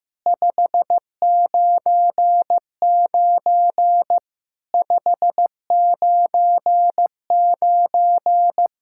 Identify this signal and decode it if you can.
Morse code, heard as a keyed tone, about 15 wpm, 700 Hz. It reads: 599 599
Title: K1USN SST QSO Practice 15wpm